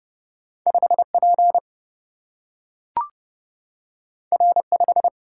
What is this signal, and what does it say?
Morse code, heard as a keyed tone, about 30 wpm, 700 Hz. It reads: 5 P R 5
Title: Single-Letter Single-Number 30wpm